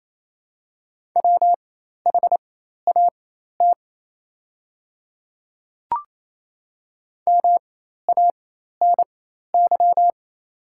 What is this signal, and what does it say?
Morse code, heard as a keyed tone, about 28 wpm, spaced as for 13 wpm, 700 Hz. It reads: WHAT MANY
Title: 4x Character Spacing - Top 100 Words - 28wpm